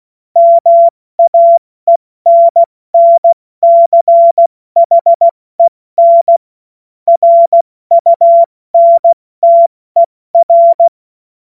Morse code, heard as a keyed tone, about 16 wpm, 680 Hz. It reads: MAENNCHEN RUNTER
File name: Morsekurs / Witze / Witz 22. Februar 2022 70 F80